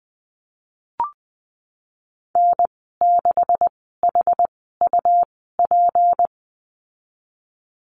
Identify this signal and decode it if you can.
Morse code, heard as a keyed tone, about 20 wpm, spaced as for 15 wpm, 700 Hz. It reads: N6HUP
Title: 154 Lesson 25g - Letter p - Call Signs 20wpm